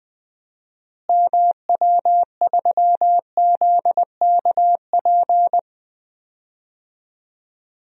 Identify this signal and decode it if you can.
Morse code, heard as a keyed tone, about 20 wpm, 700 Hz. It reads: MW3ZKP
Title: DXCC-100 Call Signs - 6 Characters - Repeated 2x - 20wpm